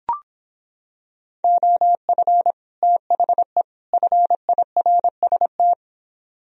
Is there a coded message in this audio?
Yes: Morse code from 1.4 to 5.7 s, heard as a keyed tone, about 26 wpm, 700 Hz. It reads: OF THE FIRST